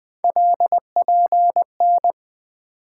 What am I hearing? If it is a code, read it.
Morse code, heard as a keyed tone, about 20 wpm, 700 Hz. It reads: LPN
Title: K1USN SST QSO Practice 20wpm